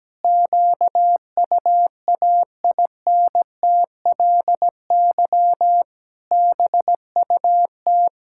Morse code, heard as a keyed tone, about 17 wpm, 700 Hz. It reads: QUAINTLY BUT
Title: Pangram Challenge v2 - 17wpm